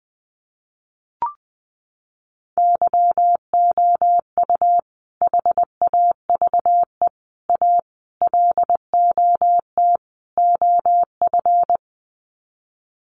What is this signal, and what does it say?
Morse code, heard as a keyed tone, about 19 wpm, 700 Hz. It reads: YOU HAVE A LOT OF
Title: Sets of 5 Words Encore 20wpm